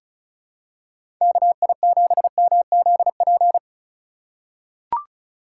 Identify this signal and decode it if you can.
Morse code, heard as a keyed tone, about 36 wpm, 700 Hz. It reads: KI7MZP
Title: USA Call Signs - 6 Characters - 2x Character Spacing - 35wpm